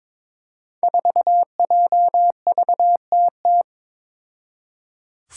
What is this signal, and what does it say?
Morse code, heard as a keyed tone, about 22 wpm, 700 Hz. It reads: 4JVTT
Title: Five Challenging Letter-Number Combinations 22wpm